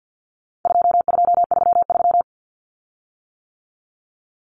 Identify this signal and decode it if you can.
Morse code, heard as a keyed tone, about 50 wpm, 700 Hz. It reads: 2233
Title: Common Cut Numbers - 4-Digit 50wpm